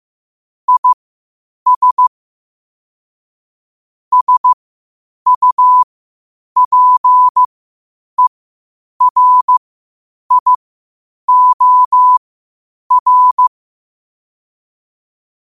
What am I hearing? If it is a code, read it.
Morse code, heard as a keyed tone, about 15 wpm, spaced as for 8 wpm, 1000 Hz. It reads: IS SUPERIOR
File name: qotdfiles / 8wpm